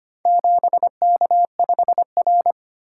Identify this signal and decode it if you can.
Morse code, heard as a keyed tone, about 25 wpm, 700 Hz. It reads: 7K5R
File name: practice / 25wpm / Four Letter-Number Combinations - 25wpm